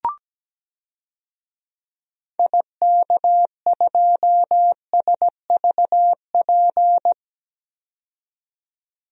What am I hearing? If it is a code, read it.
Morse code, heard as a keyed tone, about 17 wpm, 700 Hz. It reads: IK2SVP
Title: DXCC-100 Call Signs - 6 Characters - 17wpm